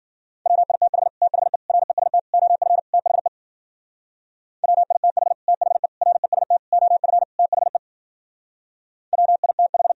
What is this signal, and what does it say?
Morse code, heard as a keyed tone, about 51 wpm, 700 Hz. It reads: WITH THE REST OF THE WITH THE REST OF THE WITH